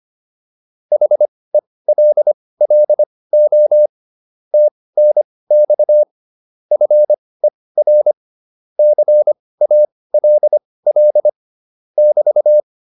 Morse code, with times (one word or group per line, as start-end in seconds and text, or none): 0.9-3.9 HELLO
4.5-6.0 TNX
6.7-8.1 FER
8.8-11.3 CALL
12.0-12.6 <BT>